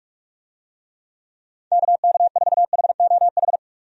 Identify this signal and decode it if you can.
Morse code, heard as a keyed tone, about 46 wpm, 700 Hz. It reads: KK4HOH
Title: USA Call Signs - 6 Characters - 45wpm